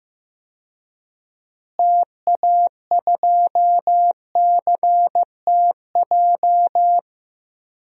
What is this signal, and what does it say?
Morse code, heard as a keyed tone, about 15 wpm, 700 Hz. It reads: TA2CTJ
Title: DXCC-100 Call Signs - 6 Characters - Repeated 2x - 15wpm